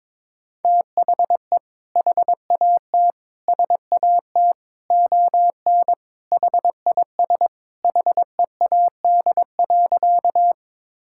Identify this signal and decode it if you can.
Morse code, heard as a keyed tone, about 22 wpm, 700 Hz. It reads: THE HAT SAT ON HIS HEAD.
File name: practice / 22wpm / Sentences with 2-Word Phrases - 22wpm